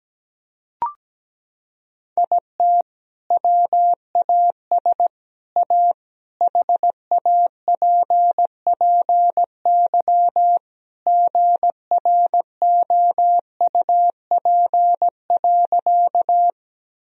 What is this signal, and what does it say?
Morse code, heard as a keyed tone, about 17 wpm, 700 Hz. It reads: IT WAS A HAPPY GROUP.
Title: Sentences from Top 1000 Words - Review Part 1 - 17wpm